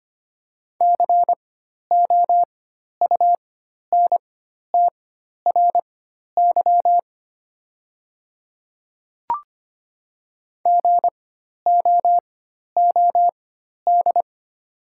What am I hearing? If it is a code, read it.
Morse code, heard as a keyed tone, about 25 wpm, spaced as for 12 wpm, 700 Hz. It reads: COUNTRY GOOD
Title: 4x Character Spacing - Top 300 Words - 25wpm